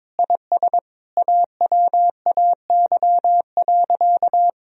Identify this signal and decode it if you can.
Morse code, heard as a keyed tone, about 21 wpm, 700 Hz. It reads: IS AWAY.
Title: Sentences from Top 1000 Words - Review Part 2 - 22wpm